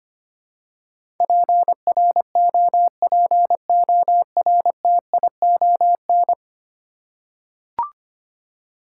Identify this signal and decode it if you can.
Morse code, heard as a keyed tone, about 25 wpm, 700 Hz. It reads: PROPORTION
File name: 10 Letter Words - First 3 Letters Given - 25wpm